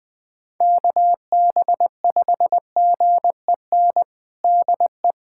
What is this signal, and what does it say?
Morse code, heard as a keyed tone, about 21 wpm, 700 Hz. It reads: KB5GEN DE
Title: USA Callsign DE Callsign - 6 Characters - 20wpm